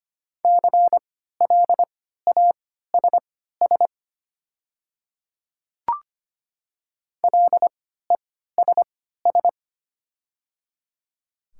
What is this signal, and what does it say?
Morse code, heard as a keyed tone, about 25 wpm, spaced as for 14 wpm, 690 Hz. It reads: CLASS LESS
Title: 3x Character Spacing - Top 500 Words - 25wpm